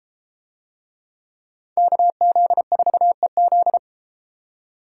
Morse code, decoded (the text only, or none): KZ4EZ